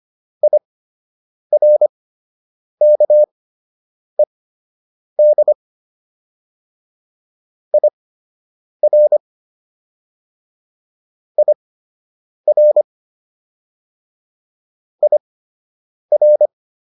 Morse code, heard as a keyed tone, about 25 wpm, spaced as for 8 wpm, 600 Hz. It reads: IRKED IR IR IR